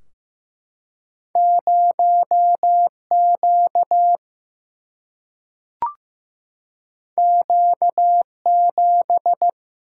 Morse code, heard as a keyed tone, about 15 wpm, 700 Hz. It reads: 0Q Q7
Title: Single-Letter Single-Number 15wpm